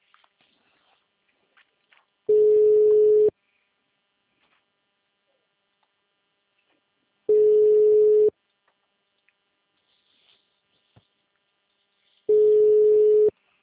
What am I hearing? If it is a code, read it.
Ringback tone (Europe, 425 Hz).